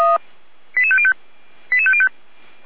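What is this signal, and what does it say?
Five-tone selective call (ZVEI), heard as quick runs of five tones, tones of about 70 ms. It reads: 88505 88505
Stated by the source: ZVEI five-tone call received on SDR.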